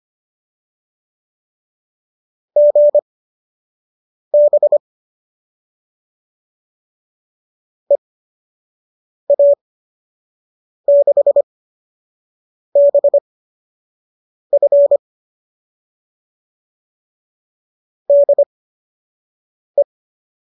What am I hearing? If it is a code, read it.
Morse code, heard as a keyed tone, about 25 wpm, spaced as for 6 wpm, 570 Hz. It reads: GB EA6BF DE